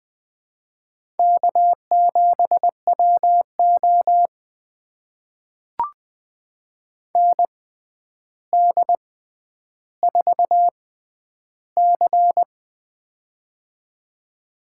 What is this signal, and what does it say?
Morse code, heard as a keyed tone, about 20 wpm, spaced as for 10 wpm, 700 Hz. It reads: K7WO ND4C
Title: USA Call Signs - 4 Characters - 6x Character Spacing - 20wpm